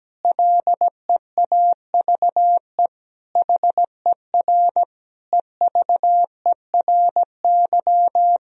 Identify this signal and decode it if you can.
Morse code, heard as a keyed tone, about 17 wpm, 700 Hz. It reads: LEAVE HER EVERY
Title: Sentences from Top 1000 Words - Review Part 1 - 17wpm